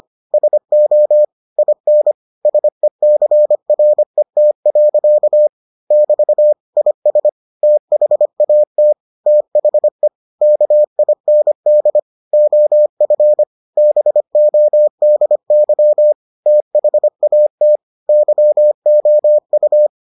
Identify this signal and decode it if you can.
Morse code, heard as a keyed tone, about 25 wpm, 600 Hz. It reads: SO IN SECRET. <BT> IS THAT THE KIND OF BODY THAT YOU